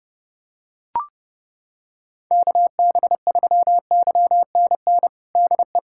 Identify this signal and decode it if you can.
Morse code, heard as a keyed tone, about 29 wpm, 700 Hz. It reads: KB3YNN DE
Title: USA Callsign DE Callsign - 6 Characters - 30wpm